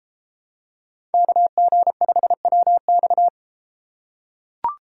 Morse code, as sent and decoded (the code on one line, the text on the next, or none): -.- --. ..... .-- -..-
KG5WX